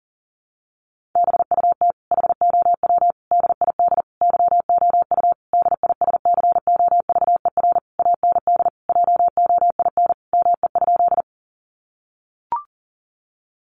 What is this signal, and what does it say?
Morse code, heard as a keyed tone, about 40 wpm, 700 Hz. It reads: BUT HOW DID YOU DISCOVER AND JOIN ME?
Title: Sentences from Top 1400 Words 40wpm